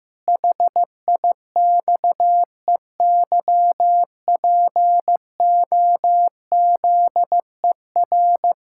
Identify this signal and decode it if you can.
Morse code, heard as a keyed tone, about 15 wpm, 700 Hz. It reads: HIXEYPOZER
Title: Pangram Challenge v2 - 15wpm